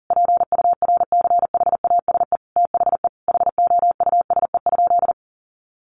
Morse code, heard as a keyed tone, about 40 wpm, 700 Hz. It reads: PURCHASE THE HOUSE?
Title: Sentences from Top 800 Words 40wpm